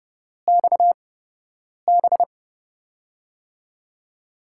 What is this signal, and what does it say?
Morse code, heard as a keyed tone, about 28 wpm, spaced as for 8 wpm, 700 Hz. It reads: XB
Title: USA Call Signs - 4 Characters - 8x Character Spacing - 30wpm